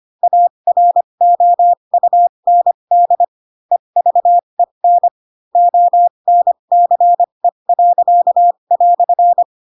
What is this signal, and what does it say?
Morse code, heard as a keyed tone, about 25 wpm, spaced as for 22 wpm, 700 Hz. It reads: AROUND EVEN ONCE."
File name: Pinocchio / Pinocchio ch4 18 25 ik1hge